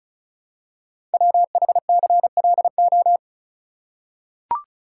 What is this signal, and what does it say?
Morse code, heard as a keyed tone, about 36 wpm, 700 Hz. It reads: WHCLO
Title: Code Groups - 5 Letter:Numbers - 35wpm